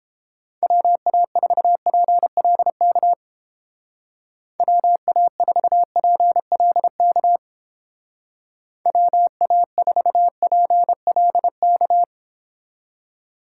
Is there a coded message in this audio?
Yes: Morse code from 0.6 to 12.0 s, heard as a keyed tone, about 28 wpm, 700 Hz. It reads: WA4PLK WA4PLK WA4PLK